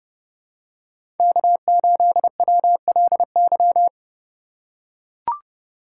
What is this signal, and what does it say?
Morse code, heard as a keyed tone, about 30 wpm, 700 Hz. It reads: K8WLY
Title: USA Call Signs - 5 Characters - 4x Character Spacing - 30wpm